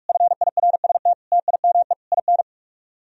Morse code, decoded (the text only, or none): FIRST TIME IN